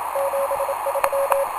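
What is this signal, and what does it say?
Morse code, heard as a keyed tone, about 27 wpm, 550 Hz, with noise. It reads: 73
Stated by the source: Portable QRP contact on a home-made 3 watt transmitter with a long dipole, received by a station running home-made 1.2 watts, with key clicks and hard keying in question.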